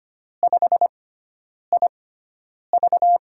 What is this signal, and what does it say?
Morse code, heard as a keyed tone, about 25 wpm, spaced as for 9 wpm, 700 Hz. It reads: 5IV
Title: USA Call Signs - 5 Characters - 6x Character Spacing - 25wpm